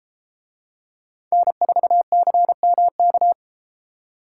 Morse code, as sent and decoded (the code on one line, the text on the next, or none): -. ....- -.-. -- -.-
N4CMK